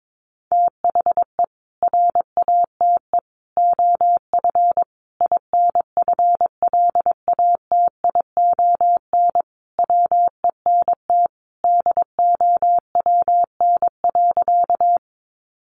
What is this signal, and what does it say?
Morse code, heard as a keyed tone, about 22 wpm, 700 Hz. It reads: THE RATE OF INFLATION WENT DOWN.
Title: Sentences from Top 1100 Words 22wpm